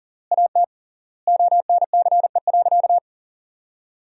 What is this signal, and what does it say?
Morse code, heard as a keyed tone, about 40 wpm, 700 Hz. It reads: AT ONCE.
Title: 3x Word-Spacing - Sentences from Top 300 Words - 40wpm